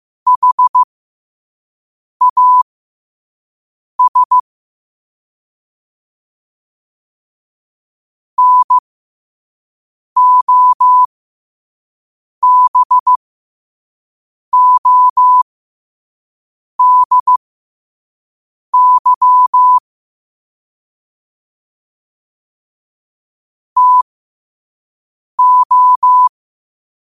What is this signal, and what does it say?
Morse code, heard as a keyed tone, about 15 wpm, spaced as for 5 wpm, 1000 Hz. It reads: HAS NOBODY TO
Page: Quotes for Fri, 15 Aug 2025 in Morse Code at 5 words per minute.